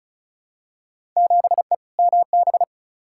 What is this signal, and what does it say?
Morse code, heard as a keyed tone, about 36 wpm, 690 Hz. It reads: 7E MB
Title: Field Day Exchanges Encore - 35wpm